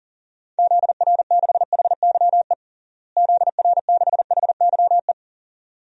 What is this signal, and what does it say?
Morse code, heard as a keyed tone, about 40 wpm, 700 Hz. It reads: ZR6HYE ZR6HYE
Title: DXCC-100 Call Signs - 6 Characters - Repeated 2x - 40wpm